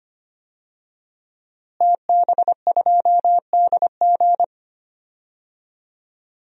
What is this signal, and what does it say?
Morse code, heard as a keyed tone, about 25 wpm, 700 Hz. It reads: TB2DG
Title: DXCC-100 Call Signs - 5 Characters - Repeated 2x - 25wpm